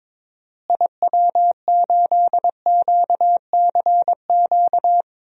Morse code, heard as a keyed tone, about 22 wpm, 700 Hz. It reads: IW8QCQ
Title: DXCC-100 Call Signs - 6 Characters - Repeated 3x - 22wpm